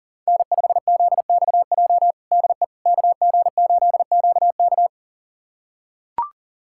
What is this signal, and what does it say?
Morse code, heard as a keyed tone, about 40 wpm, 700 Hz. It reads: N5ZXJ DE KG8QK
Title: USA Callsign DE Callsign - 5 Characters - 40wpm